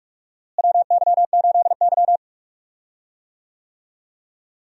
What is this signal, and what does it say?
Morse code, heard as a keyed tone, about 41 wpm, 700 Hz. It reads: WY8Y
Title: USA Call Signs - 4 Characters - Repeated 2x - 45wpm